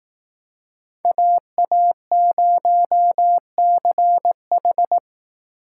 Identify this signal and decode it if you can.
Morse code, heard as a keyed tone, about 18 wpm, 700 Hz. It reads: AA0CH
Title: Speed Racing - USA 2x2 Call Signs 15wpm